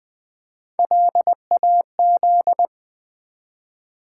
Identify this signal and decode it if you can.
Morse code, heard as a keyed tone, about 20 wpm, 700 Hz. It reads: LAZ
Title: Build A Word - 1 Letter At A Time - 4-Letter Words - 20wpm